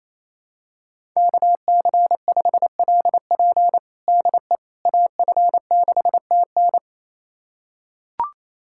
Morse code, heard as a keyed tone, about 28 wpm, 700 Hz. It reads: KC5LP DE AF6TN
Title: USA Callsign DE Callsign - 5 Characters - 28wpm